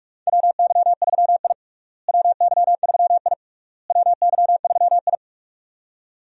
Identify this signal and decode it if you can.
Morse code, heard as a keyed tone, about 46 wpm, 700 Hz. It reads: WY3I WY3I WY3I
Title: USA Call Signs - 4 Characters - Repeated 3x - 45wpm